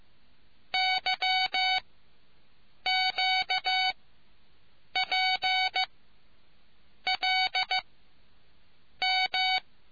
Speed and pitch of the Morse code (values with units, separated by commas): 15 wpm, spaced as for 6 wpm, 750 Hz